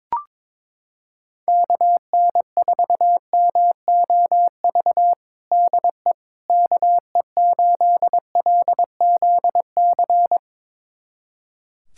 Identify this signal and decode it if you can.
Morse code, heard as a keyed tone, about 22 wpm, 700 Hz. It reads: KN4MOV DE KE8LZC
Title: USA Callsign DE Callsign - 6 Characters - 22wpm